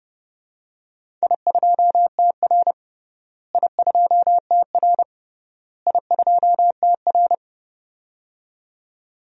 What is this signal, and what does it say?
Morse code, heard as a keyed tone, about 32 wpm, 700 Hz. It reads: I2TR I2TR I2TR
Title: DXCC-100 Call Signs - 4 Characters - Repeated 3x - 30wpm